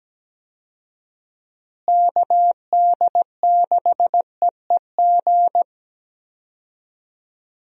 Morse code, heard as a keyed tone, about 17 wpm, 700 Hz. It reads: KD6EEG
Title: USA Call Signs - 6 Characters - Repeated 2x - 17wpm